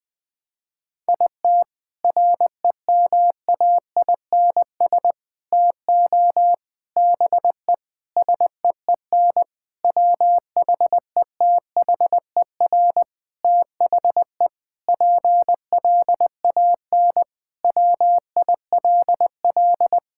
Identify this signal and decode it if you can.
Morse code, heard as a keyed tone, about 21 wpm, 700 Hz. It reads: IT REMAINS TO BE SEEN WHETHER THE PLAN WILL
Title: Sentences with 7-Word Phrases - Spoken Beginning - 20wpm